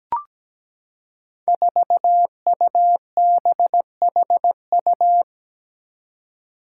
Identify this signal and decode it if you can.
Morse code, heard as a keyed tone, about 17 wpm, 700 Hz. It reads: 4UBHU
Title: Five Challenging Letter-Number Combinations 17wpm